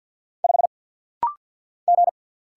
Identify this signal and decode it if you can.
Morse code, heard as a keyed tone, about 50 wpm, 700 Hz. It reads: 5 G
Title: Single Letter-Number - Rapid-Fire 50wpm